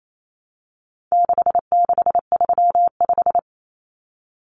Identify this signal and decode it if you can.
Morse code, heard as a keyed tone, about 28 wpm, 700 Hz. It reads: 6635
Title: Numbers - 4-Digit 28wpm